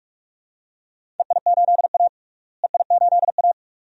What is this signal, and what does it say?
Morse code, heard as a keyed tone, about 43 wpm, 700 Hz. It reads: EI8A EI8A
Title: DXCC-100 Call Signs - 4 Characters - Repeated 2x - 45wpm